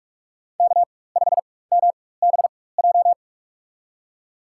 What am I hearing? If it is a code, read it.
Morse code, heard as a keyed tone, about 46 wpm, spaced as for 21 wpm, 700 Hz. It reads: K5MBJ
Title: USA Call Signs - 5 Characters - 4x Character Spacing - 45wpm